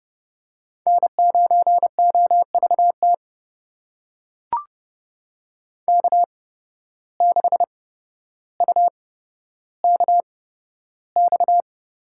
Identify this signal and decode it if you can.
Morse code, heard as a keyed tone, about 31 wpm, spaced as for 13 wpm, 700 Hz. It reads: N9OVT K6UKX
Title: USA Call Signs - 5 Characters - 8x Character Spacing - 30wpm